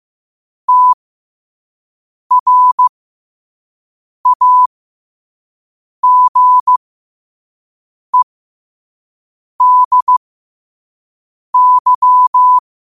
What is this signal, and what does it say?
Morse code, heard as a keyed tone, about 15 wpm, spaced as for 5 wpm, 1000 Hz. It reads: TRAGEDY
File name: qotdfiles / 5wpm